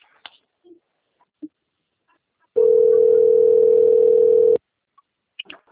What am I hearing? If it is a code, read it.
Ringback tone (North America), 440 + 480 Hz.
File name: out- - - - - .12334